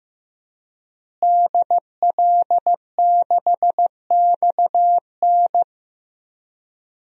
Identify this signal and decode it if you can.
Morse code, heard as a keyed tone, about 15 wpm, 700 Hz. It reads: DL6XN
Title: DXCC-100 Call Signs - 5 Characters - Repeated 3x - 15wpm